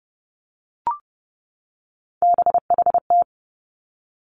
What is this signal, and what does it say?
Morse code, heard as a keyed tone, about 30 wpm, 700 Hz. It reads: BHT
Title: Three Challenging Letter-Number Combinations 30wpm